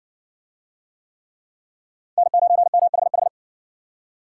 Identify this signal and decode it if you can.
Morse code, heard as a keyed tone, about 60 wpm, 700 Hz. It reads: N9MHH